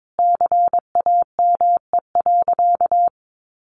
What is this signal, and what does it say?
Morse code, heard as a keyed tone, about 22 wpm, 700 Hz. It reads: CAME.
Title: Sentences from Top 1400 Words 22wpm